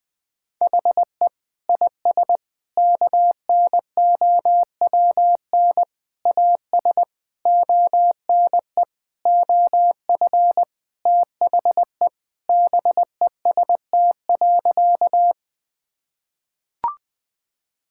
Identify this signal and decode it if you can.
Morse code, heard as a keyed tone, about 20 wpm, 700 Hz. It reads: HE IS KNOWN AS ONE OF THE BEST.
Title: Sentences with 4-Word Phrases - 20wpm